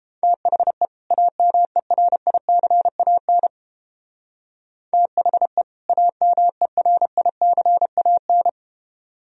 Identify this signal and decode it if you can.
Morse code, heard as a keyed tone, about 29 wpm, 700 Hz. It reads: THE AMERICAN THE AMERICAN